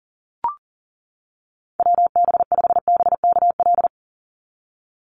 Morse code, heard as a keyed tone, about 40 wpm, 700 Hz. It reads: WB5BKL
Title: NA SOTA Chaser Callsigns 40wpm